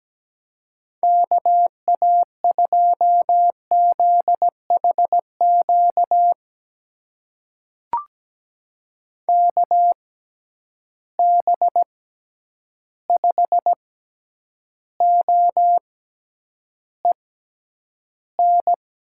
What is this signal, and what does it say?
Morse code, heard as a keyed tone, about 17 wpm, spaced as for 9 wpm, 700 Hz. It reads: KA2ZHQ KB5OEN
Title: USA Call Signs - 6 Characters - 6x Character Spacing - 17wpm